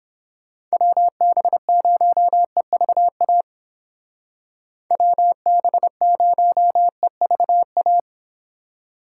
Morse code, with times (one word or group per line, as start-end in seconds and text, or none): 0.7-3.4 WB0EVA
4.9-8.0 WB0EVA